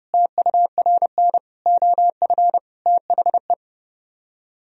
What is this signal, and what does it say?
Morse code, heard as a keyed tone, about 30 wpm, 700 Hz. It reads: TURN OF THE